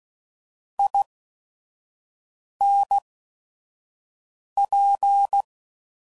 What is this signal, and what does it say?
Morse code, heard as a keyed tone, about 16 wpm, spaced as for 5 wpm, 800 Hz. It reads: INP
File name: CW Code Groups @5WPM